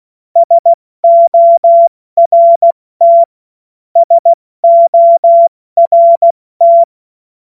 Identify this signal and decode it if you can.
Morse code, heard as a keyed tone, about 16 wpm, 680 Hz. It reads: SORT SORT
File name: Morsekurs / Witze / Witz 23 Januar 2024 70 F80